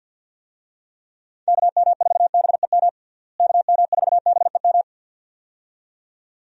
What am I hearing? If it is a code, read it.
Morse code, heard as a keyed tone, about 52 wpm, 700 Hz. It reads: KM4BEM KM4BEM
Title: USA Call Signs - 6 Characters - Repeated 2x - 50wpm